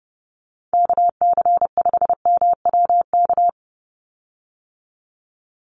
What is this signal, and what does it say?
Morse code, heard as a keyed tone, about 30 wpm, 700 Hz. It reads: KC5MWK
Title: USA Call Signs 30wpm